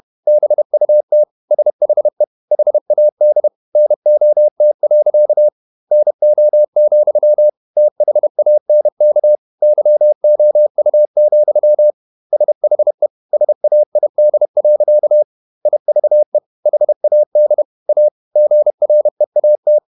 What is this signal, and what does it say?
Morse code, heard as a keyed tone, about 31 wpm, 600 Hz. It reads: BUT SHE HAD NOT. NO, THANK YOU, SHE SAID. IVE HAD A GREAT
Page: Never 31wpm | CW med Gnister